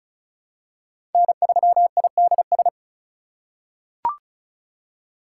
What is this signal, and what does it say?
Morse code, heard as a keyed tone, about 36 wpm, 700 Hz. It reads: N3IDS
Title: USA Call Signs - 4x Character Spacing - 35wpm